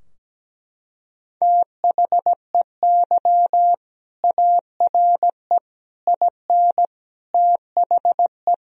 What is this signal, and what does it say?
Morse code, heard as a keyed tone, about 17 wpm, 700 Hz. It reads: THEY ARE IN THE
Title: Sentences from Top 1000 Words - Review Part 3 - 17wpm